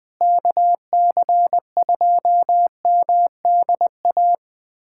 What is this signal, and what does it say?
Morse code, heard as a keyed tone, about 20 wpm, 700 Hz. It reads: KC2MDA
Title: USA Callsign DE Callsign - 6 Characters - 20wpm